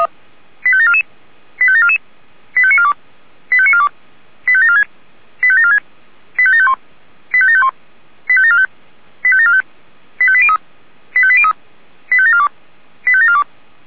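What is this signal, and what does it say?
Five-tone selective call (ZVEI), heard as quick runs of five tones, tones of about 70 ms. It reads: 85744 85744 85942 85942 85746 85746 85751 85751 85745 85745 85803 85803 85742 85742
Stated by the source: ZVEI five-tone selective call received on SDR.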